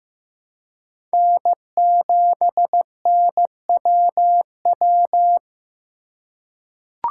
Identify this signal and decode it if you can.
Morse code, heard as a keyed tone, about 15 wpm, 700 Hz. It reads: N7NWW
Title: USA Call Signs - 5 Characters - 4x Character Spacing - 15wpm